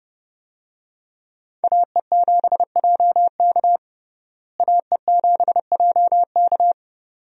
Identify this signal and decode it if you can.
Morse code, heard as a keyed tone, about 30 wpm, 700 Hz. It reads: AE7JK AE7JK
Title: USA Call Signs - 5 Characters - Repeated 2x - 30wpm